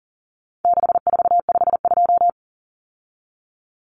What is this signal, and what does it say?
Morse code, heard as a keyed tone, about 40 wpm, 700 Hz. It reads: 6452
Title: Common Cut Numbers - 4-Digit 40wpm